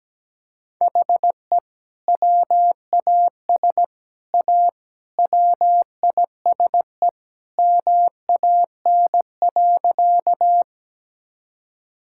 Morse code, heard as a keyed tone, about 17 wpm, 700 Hz. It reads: HE WAS A WISE MAN.
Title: Sentences from Top 1600 Words 17wpm